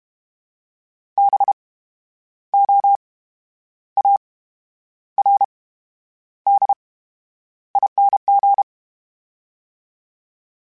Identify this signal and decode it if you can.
Morse code, heard as a keyed tone, about 32 wpm, spaced as for 10 wpm, 800 Hz. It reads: BOARDING
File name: Special INGs / Special INGs-mp3s / VerySlowToFaster Ing @30